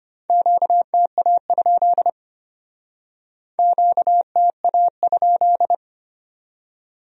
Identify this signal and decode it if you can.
Morse code, heard as a keyed tone, about 32 wpm, 700 Hz. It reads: QTA? QTA?